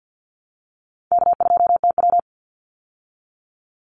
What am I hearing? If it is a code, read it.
Morse code, heard as a keyed tone, about 49 wpm, 700 Hz. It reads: K2TW
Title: Easy Call Signs Repeated 2x - 50wpm